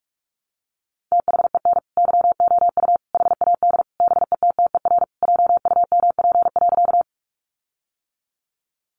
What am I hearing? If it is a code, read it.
Morse code, heard as a keyed tone, about 44 wpm, 700 Hz. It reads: THEN YOU HAD BETTER JUMP.
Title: Sentences from Top 1300 Words 45wpm